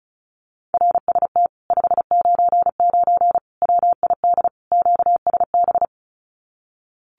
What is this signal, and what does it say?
Morse code, heard as a keyed tone, about 34 wpm, 700 Hz. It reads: RST 599 WID QSB
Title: Common QSO Phrases Encore 35wpm